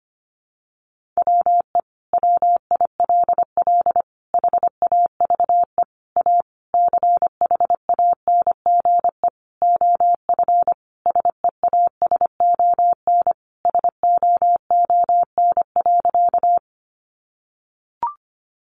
Morse code, heard as a keyed tone, about 25 wpm, 700 Hz. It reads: WE WILL HAVE A CHANGE OF SEASON SOON.
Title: Sentences from Top 1400 Words 25wpm